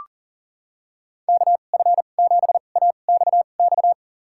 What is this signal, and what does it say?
Morse code, heard as a keyed tone, about 40 wpm, spaced as for 29 wpm, 700 Hz. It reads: KF7AXX
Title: USA Call Signs - 6 Characters - 2x Character Spacing - 40wpm